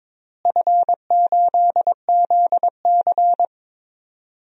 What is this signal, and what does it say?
Morse code, heard as a keyed tone, about 22 wpm, 700 Hz. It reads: F8ZC
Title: DXCC-100 Call Signs - 4 Characters - Repeated 3x - 22wpm